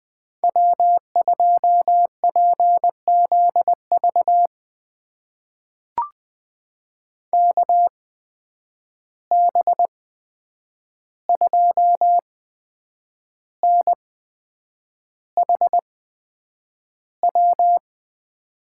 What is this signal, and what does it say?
Morse code, heard as a keyed tone, about 20 wpm, spaced as for 8 wpm, 700 Hz. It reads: W2PZV KB2NHW
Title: USA Call Signs - 8x Character Spacing - 20wpm